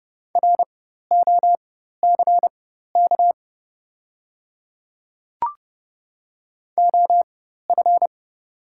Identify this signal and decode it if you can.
Morse code, heard as a keyed tone, about 30 wpm, spaced as for 14 wpm, 700 Hz. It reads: ROCK OF
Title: 4x Character Spacing - Top 500 Words - 30wpm